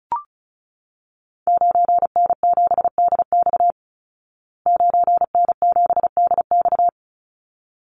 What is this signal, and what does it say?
Morse code, heard as a keyed tone, about 34 wpm, 700 Hz. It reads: 9N7DX 9N7DX
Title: Call Signs Repeated 3x - 35wpm